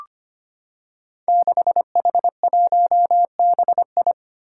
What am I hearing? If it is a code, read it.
Morse code, heard as a keyed tone, about 25 wpm, 700 Hz. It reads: 6H1BI
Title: Five Challenging Letter-Number Combinations 25wpm